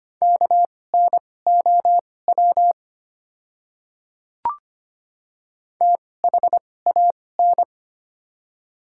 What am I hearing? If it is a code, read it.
Morse code, heard as a keyed tone, about 25 wpm, spaced as for 18 wpm, 690 Hz. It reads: KNOW THAN
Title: 2x Character Spacing - Top 100 Words - 25wpm